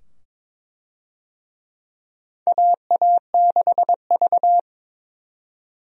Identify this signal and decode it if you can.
Morse code, heard as a keyed tone, about 22 wpm, 700 Hz. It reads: AA6V
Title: USA Call Signs - 4 Characters - 22wpm